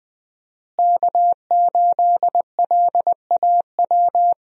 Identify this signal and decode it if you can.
Morse code, heard as a keyed tone, about 20 wpm, 700 Hz. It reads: K8LAW
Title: USA Call Signs - 5 Characters - Repeated 3x - 20wpm